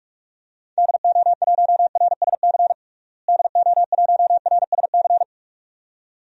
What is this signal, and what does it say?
Morse code, heard as a keyed tone, about 41 wpm, 700 Hz. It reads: DO1RSC DO1RSC
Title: DXCC-100 Call Signs - 6 Characters - Repeated 2x - 45wpm